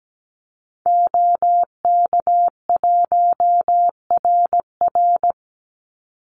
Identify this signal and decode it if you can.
Morse code, heard as a keyed tone, about 17 wpm, 700 Hz. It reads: OK1RR
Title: Call Signs Repeated 2x - 17wpm